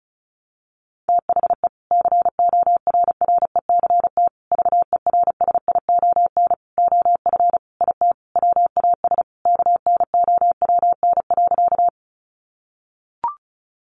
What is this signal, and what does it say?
Morse code, heard as a keyed tone, about 34 wpm, 700 Hz. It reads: THE CORRECT VERSION OF IT WAS KNOWN.
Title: Sentences from Top 1400 Words 35wpm